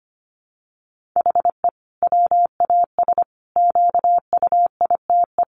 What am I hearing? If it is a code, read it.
Morse code, heard as a keyed tone, about 25 wpm, 700 Hz. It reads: HE WAS QUITE
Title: Sentences from Top 1400 Words 25wpm